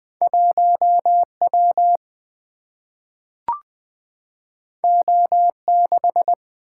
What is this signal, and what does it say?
Morse code, heard as a keyed tone, about 20 wpm, 700 Hz. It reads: 1 W O 6
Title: Single-Letter Single-Number 20wpm